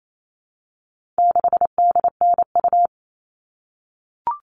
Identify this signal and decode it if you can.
Morse code, heard as a keyed tone, about 28 wpm, 700 Hz. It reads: 6DNU
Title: All Cut Numbers - 4-Digit 28wpm